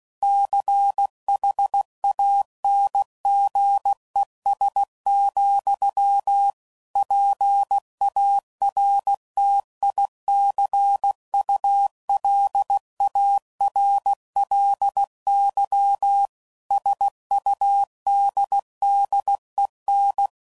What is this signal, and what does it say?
Morse code, heard as a keyed tone, about 16 wpm, 800 Hz. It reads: CHANGES, PARTICULARLY SUDDEN
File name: CW Code Groups @15WPM / Technical Subjects / Reactance